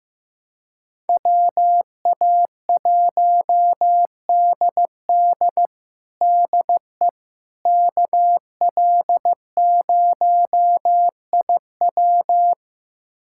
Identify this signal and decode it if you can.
Morse code, heard as a keyed tone, about 15 wpm, 700 Hz. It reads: WA1DD DE KL0IW
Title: USA Callsign DE Callsign - 5 Characters - 15wpm